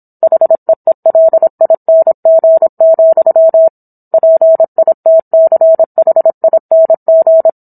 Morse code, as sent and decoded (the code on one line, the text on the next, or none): .... . .-.. .. -. --. --..-- / .--. .. - -.-. .... .. -. --.
HEELING, PITCHING